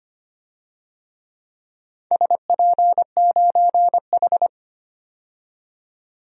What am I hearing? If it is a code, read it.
Morse code, heard as a keyed tone, about 25 wpm, 700 Hz. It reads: SP9H
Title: DXCC-100 Call Signs - 4 Characters - 25wpm